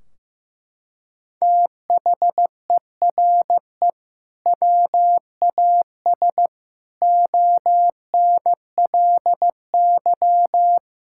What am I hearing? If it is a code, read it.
Morse code, heard as a keyed tone, about 14 wpm, 700 Hz. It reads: THERE WAS ONLY
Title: Sentences from Top 1600 Words 15wpm